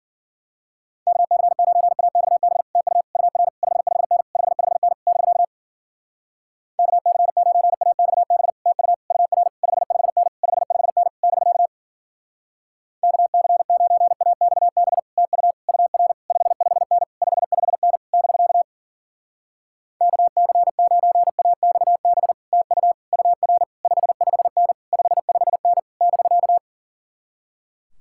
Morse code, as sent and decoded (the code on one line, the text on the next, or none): -.- -.-. ----. .- -..- -... / - ..- / ..- .-. / ..... ..... -. / ..... ..... -. / -...-.- / -.- -.-. ----. .- -..- -... / - ..- / ..- .-. / ..... ..... -. / ..... ..... -. / -...-.- / -.- -.-. ----. .- -..- -... / - ..- / ..- .-. / ..... ..... -. / ..... ..... -. / -...-.- / -.- -.-. ----. .- -..- -... / - ..- / ..- .-. / ..... ..... -. / ..... ..... -. / -...-.-
KC9AXB TU UR 55N 55N <BK> KC9AXB TU UR 55N 55N <BK> KC9AXB TU UR 55N 55N <BK> KC9AXB TU UR 55N 55N <BK>